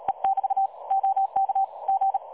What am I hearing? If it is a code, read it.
Morse code, heard as a keyed tone, about 37 wpm, 780 Hz. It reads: E<BT> OK G
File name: dlazaawansowanych / nagraniazpasma / websdr recording 3565.0kHz